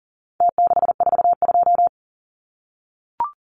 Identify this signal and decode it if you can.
Morse code, heard as a keyed tone, about 40 wpm, 700 Hz. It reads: T642
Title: Numbers - Contest Style - 4-Digit 40wpm